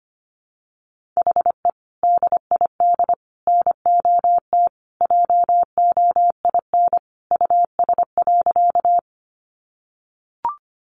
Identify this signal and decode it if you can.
Morse code, heard as a keyed tone, about 25 wpm, 700 Hz. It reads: HE DID NOT JOIN US.
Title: Sentences from Top 1400 Words 25wpm